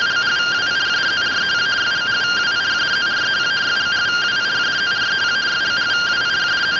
Two-tone teleprinter signal, about 170 Hz shift.